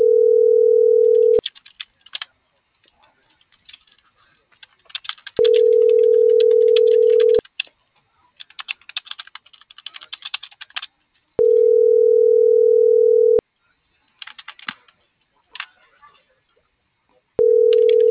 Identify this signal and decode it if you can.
Ringback tone (North America), 440 + 480 Hz.